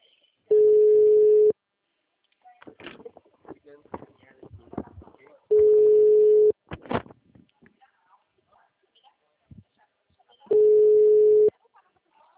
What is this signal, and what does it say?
Ringback tone (Europe, 425 Hz), with speech.